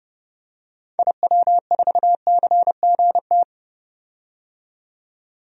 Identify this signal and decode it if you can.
Morse code, heard as a keyed tone, about 32 wpm, 700 Hz. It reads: IW4CGT
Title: DXCC-100 Call Signs - 6 Characters - 30wpm